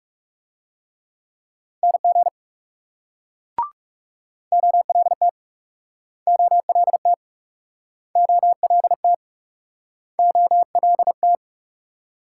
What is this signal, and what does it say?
Morse code, heard as a keyed tone, about 38 wpm, 700 Hz. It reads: NG OLT OLT OLT OLT